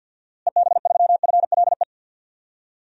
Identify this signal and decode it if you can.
Morse code, heard as a keyed tone, about 50 wpm, 700 Hz. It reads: EB3FLE
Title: DXCC-100 Call Signs - 6 Characters - Repeated 2x - 50wpm